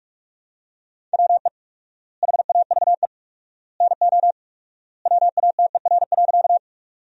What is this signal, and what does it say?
Morse code, heard as a keyed tone, about 47 wpm, 700 Hz. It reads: WE HAVE NO WATER.
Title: 4x Word-Spacing - Sentences from Top 100 Words - 45wpm